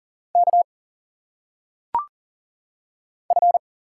Morse code, heard as a keyed tone, about 40 wpm, 700 Hz. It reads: K F
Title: Single Letters - Phonetic Pronunciation 40wpm